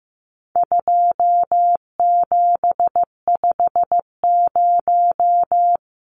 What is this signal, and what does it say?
Morse code, heard as a keyed tone, about 15 wpm, 700 Hz. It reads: 2750
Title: Numbers - 4-Digit 15wpm